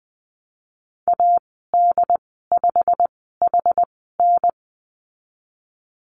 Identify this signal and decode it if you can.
Morse code, heard as a keyed tone, about 20 wpm, spaced as for 15 wpm, 700 Hz. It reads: AD5HN
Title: 088 Lesson 15g - Number 5 - Call Signs 20wpm